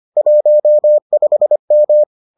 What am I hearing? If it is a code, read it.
Morse code, heard as a keyed tone, about 25 wpm, 600 Hz. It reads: 15M